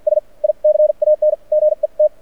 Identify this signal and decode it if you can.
Morse code, heard as a keyed tone, about 50 wpm, spaced as for 42 wpm, 600 Hz. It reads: S I CAN GET
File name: archived / CRSnet